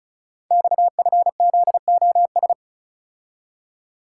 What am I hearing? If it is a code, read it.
Morse code, heard as a keyed tone, about 36 wpm, 700 Hz. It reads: XFZOS
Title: Code Groups - 5 Letters - 35wpm